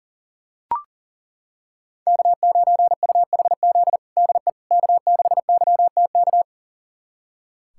Morse code, heard as a keyed tone, about 40 wpm, 700 Hz. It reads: K9UHZ DE K6YTK
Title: USA Callsign DE Callsign - 5 Characters - 40wpm